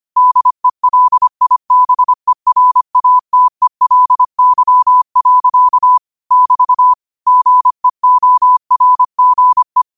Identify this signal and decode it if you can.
Morse code, heard as a keyed tone, about 25 wpm, 1000 Hz. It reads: DELIBERATELY. <BT> GEORGE